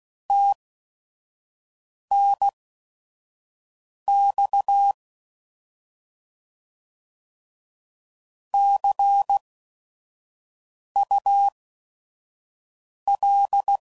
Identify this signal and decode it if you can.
Morse code, heard as a keyed tone, about 16 wpm, spaced as for 5 wpm, 800 Hz. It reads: TNX CUL
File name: Mixed / Abreviations / Abreviations